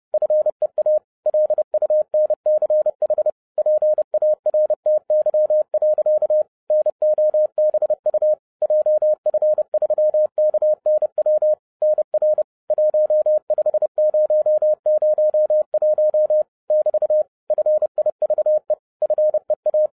Morse code, heard as a keyed tone, about 30 wpm, 610 Hz. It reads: FEA LUNCH PARTY. NOBU JF3KNW NR 15001 <BT> FIVE FEA